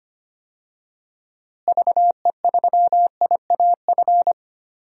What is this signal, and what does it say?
Morse code, heard as a keyed tone, about 25 wpm, 700 Hz. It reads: VE3IAF
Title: DXCC-100 Call Signs - 6 Characters - Repeated 2x - 25wpm